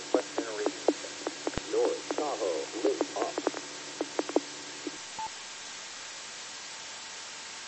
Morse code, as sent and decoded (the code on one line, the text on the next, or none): none